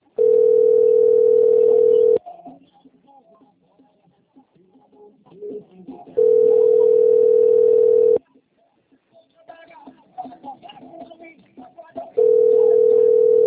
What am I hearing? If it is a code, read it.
Ringback tone (North America), 440 + 480 Hz.